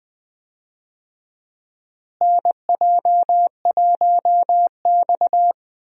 Morse code, heard as a keyed tone, about 19 wpm, 700 Hz. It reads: NJ1X
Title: USA Call Signs - 4 Characters - Repeated 3x - 20wpm